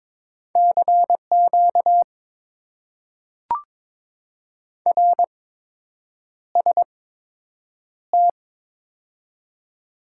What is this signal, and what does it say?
Morse code, heard as a keyed tone, about 22 wpm, spaced as for 8 wpm, 700 Hz. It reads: CQ RST
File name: Top 90 QSO Elements - 8x Character Spacing - 22wpm